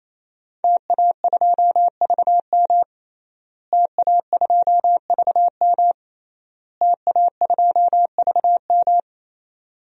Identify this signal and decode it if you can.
Morse code, heard as a keyed tone, about 28 wpm, 700 Hz. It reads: TA2VM TA2VM TA2VM
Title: DXCC-100 Call Signs - 5 Characters - Repeated 3x - 28wpm